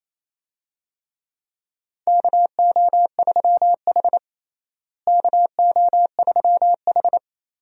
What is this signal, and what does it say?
Morse code, heard as a keyed tone, about 27 wpm, 700 Hz. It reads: KO3H KO3H
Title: USA Call Signs - 4 Characters - Repeated 3x - 28wpm